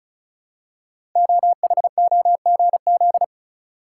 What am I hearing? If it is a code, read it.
Morse code, heard as a keyed tone, about 36 wpm, 700 Hz. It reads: OHOGZ
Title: Code Groups - 5 Letters - 35wpm